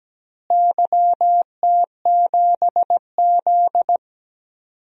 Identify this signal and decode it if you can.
Morse code, heard as a keyed tone, about 17 wpm, 700 Hz. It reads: YT7Z
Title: DXCC-100 Call Signs - 4 Characters - Repeated 2x - 17wpm